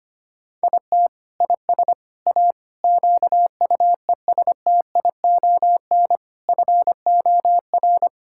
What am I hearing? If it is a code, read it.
Morse code, heard as a keyed tone, about 25 wpm, 700 Hz. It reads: IT IS A QUESTION FOR
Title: Sentences from Top 1000 Words - Review Part 6 - 25wpm